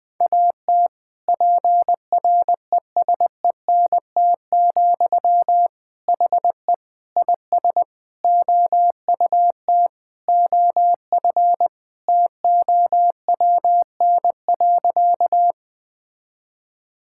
Morse code, heard as a keyed tone, about 20 wpm, 700 Hz. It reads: AT PRESENT, HE IS OUT OF TOWN.
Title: Sentences from Top 1000 Words - Review Part 5 - 20wpm